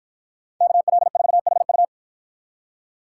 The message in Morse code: KL4HU

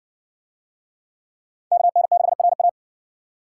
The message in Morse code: KN6RA

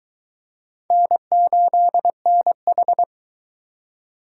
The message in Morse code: N8NH

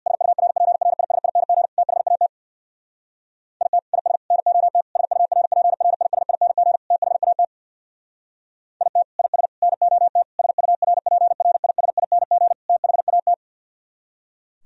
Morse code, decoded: SURPRISING THAT IT IS NOT SURPRISING THAT IT IS NOT SURPRISING THAT

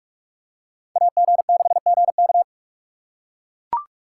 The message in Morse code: AG6GK

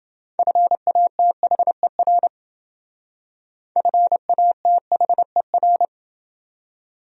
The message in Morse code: FATHER FATHER

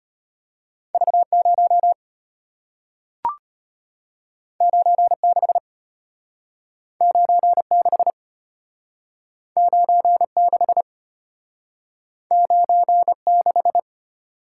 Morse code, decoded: V0 96 96 96 96

